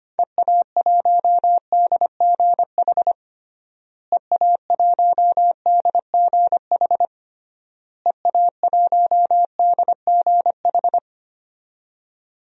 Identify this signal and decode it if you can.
Morse code, heard as a keyed tone, about 25 wpm, 700 Hz. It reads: EA1DGH EA1DGH EA1DGH
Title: DXCC-100 Call Signs - 6 Characters - Repeated 3x - 25wpm